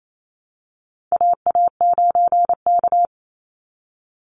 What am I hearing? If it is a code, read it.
Morse code, heard as a keyed tone, about 28 wpm, 700 Hz. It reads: AA9K
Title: Call Signs Repeated 3x - 28wpm